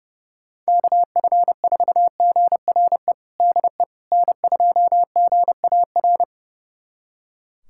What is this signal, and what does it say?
Morse code, heard as a keyed tone, about 30 wpm, 700 Hz. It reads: KF4GRE DE N2GAR
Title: USA Callsign DE Callsign 30wpm